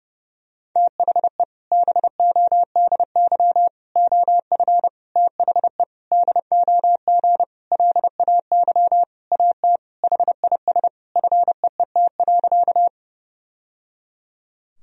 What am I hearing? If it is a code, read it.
Morse code, heard as a keyed tone, about 30 wpm, 700 Hz. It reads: THE BODY OF THE DOG LAY AT HIS FEET.
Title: Sentences from Top 1000 Words - Review Part 2 - 30wpm